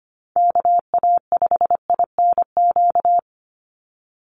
Morse code, heard as a keyed tone, about 25 wpm, 700 Hz. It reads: KA5INQ
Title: USA Call Signs 25wpm